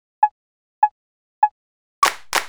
There is no code.